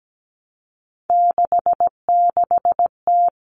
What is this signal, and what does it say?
Morse code, heard as a keyed tone, about 17 wpm, 700 Hz. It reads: 66T
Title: Common Cut Numbers - 3-Digit 17wpm